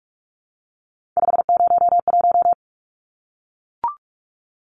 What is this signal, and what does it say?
Morse code, heard as a keyed tone, about 44 wpm, 700 Hz. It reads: 501
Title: Numbers - 3-Digit 45wpm